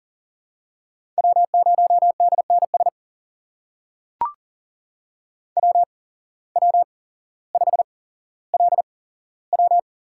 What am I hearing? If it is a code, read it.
Morse code, heard as a keyed tone, about 40 wpm, spaced as for 17 wpm, 700 Hz. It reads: W0DNS WW5LW
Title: USA Call Signs - 5 Characters - 8x Character Spacing - 40wpm